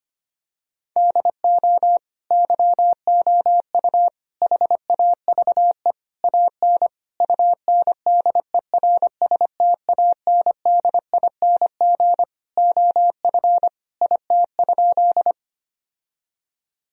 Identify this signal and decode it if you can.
Morse code, heard as a keyed tone, about 25 wpm, 700 Hz. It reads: DO YOU HAVE AN UNDERSTANDING OF IT?
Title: Sentences from Top 1000 Words - Review Part 3 - 25wpm